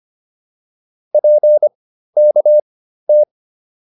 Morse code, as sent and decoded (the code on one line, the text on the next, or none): .--. -.- -
PKT